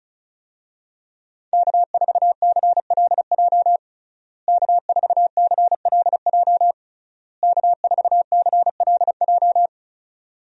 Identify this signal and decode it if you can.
Morse code, heard as a keyed tone, about 34 wpm, 700 Hz. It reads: K4CLJ K4CLJ K4CLJ
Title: USA Call Signs - 5 Characters - Repeated 3x - 35wpm